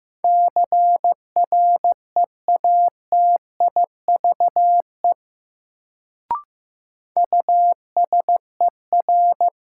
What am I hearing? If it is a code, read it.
Morse code, heard as a keyed tone, about 15 wpm, 700 Hz. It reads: CREATIVE USER
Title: Sentences from Top 1000 Words - Review Part 5 - New Word Drill - 15wpm